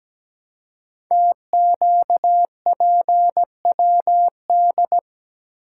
Morse code, heard as a keyed tone, about 17 wpm, 700 Hz. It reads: TQPWD
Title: Code Groups - 5 Letters - 17wpm